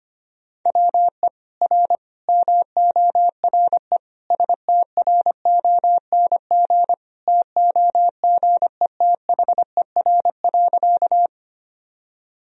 Morse code, heard as a keyed tone, about 25 wpm, 700 Hz. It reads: WE R MORE STRONG TOGETHER.
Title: Sentences from Top 500 Words and Abbreviations 25wpm